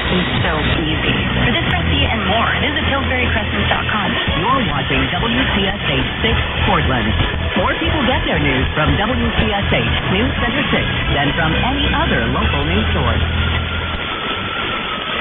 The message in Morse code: none

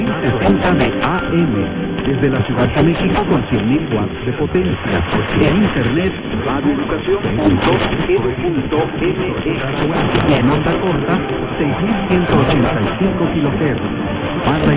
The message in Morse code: none

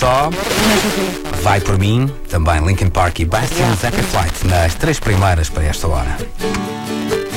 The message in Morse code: none

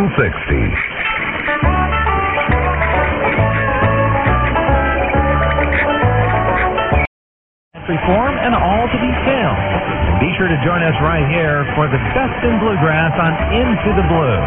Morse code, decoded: none

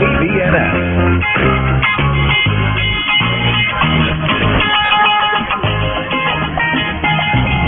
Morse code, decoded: none

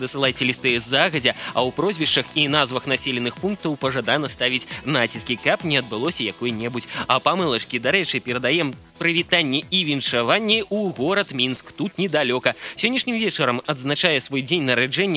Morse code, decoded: none

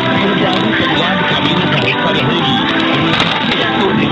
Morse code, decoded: none

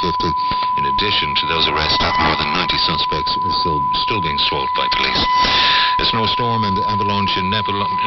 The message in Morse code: none

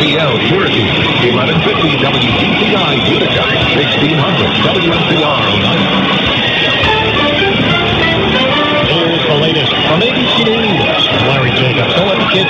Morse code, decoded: none

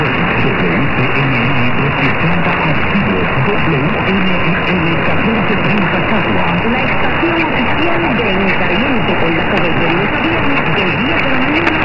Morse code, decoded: none

